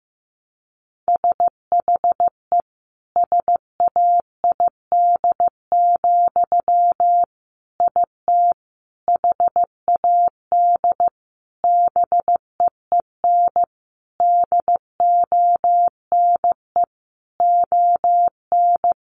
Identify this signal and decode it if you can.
Morse code, heard as a keyed tone, about 14 wpm, 700 Hz. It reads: SHE SAID, IT HAD BEEN DONE ON
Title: Sentences from Top 800 Words 15wpm